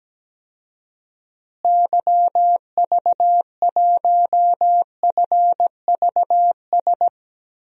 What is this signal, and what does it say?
Morse code, heard as a keyed tone, about 17 wpm, 700 Hz. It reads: YV1FVS
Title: DXCC-100 Call Signs - 6 Characters - Repeated 2x - 17wpm